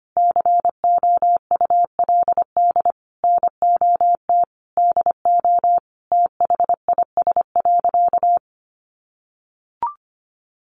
Morse code, heard as a keyed tone, about 25 wpm, 700 Hz. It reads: COULD NOT DO THIS.
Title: Sentences from Top 1100 Words 25wpm